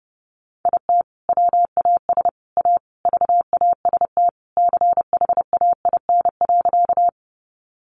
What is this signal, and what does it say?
Morse code, heard as a keyed tone, about 30 wpm, 700 Hz. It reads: IT WAS A VAST CHAIN.
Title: Sentences from Top 1500 Words 30wpm